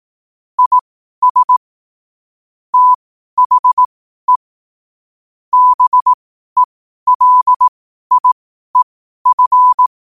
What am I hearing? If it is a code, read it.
Morse code, heard as a keyed tone, about 18 wpm, spaced as for 12 wpm, 1000 Hz. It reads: IS THE BELIEF
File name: qotdfiles / 12wpm